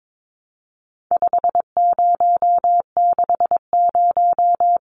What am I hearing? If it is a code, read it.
Morse code, heard as a keyed tone, about 22 wpm, 700 Hz. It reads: 5060
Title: Numbers - 4-Digit 22wpm